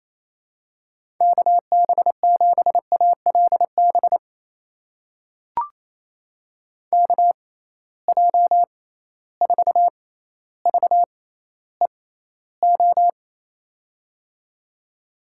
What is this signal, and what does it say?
Morse code, heard as a keyed tone, about 28 wpm, spaced as for 14 wpm, 700 Hz. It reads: KB7ALB KJ4VEO
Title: USA Call Signs - 6 Characters - 6x Character Spacing - 28wpm